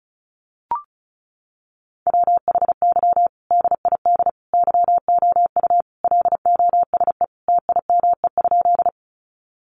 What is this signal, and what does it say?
Morse code, heard as a keyed tone, about 34 wpm, 700 Hz. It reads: WHY DID YOU LOSE TIME?
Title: Sentences from Top 1100 Words 35wpm